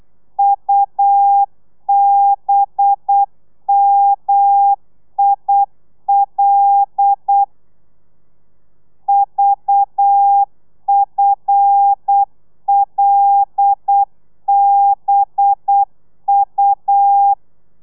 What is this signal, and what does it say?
Morse code, heard as a keyed tone, about 8 wpm, 800 Hz. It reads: UBMIL VFLBU
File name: telegrafia / i2tfj / Lez5 35